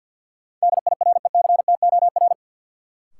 Morse code, heard as a keyed tone, about 50 wpm, 700 Hz. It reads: DIRECTOR
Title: Sets of 3 Words - New Word Drill - 50wpm